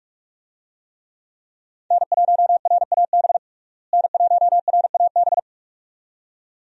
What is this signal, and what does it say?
Morse code, heard as a keyed tone, about 46 wpm, 700 Hz. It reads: N1RAB N1RAB
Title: USA Call Signs - 5 Characters - Repeated 2x - 45wpm